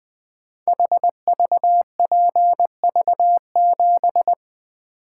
Morse code, heard as a keyed tone, about 20 wpm, 700 Hz. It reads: HVPV7
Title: Code Groups - 5 Letter:Numbers - 20wpm